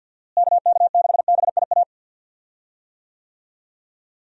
Morse code, heard as a keyed tone, about 50 wpm, 700 Hz. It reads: KK6BIA
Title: USA Call Signs - 6 Characters - Repeated 3x - 50wpm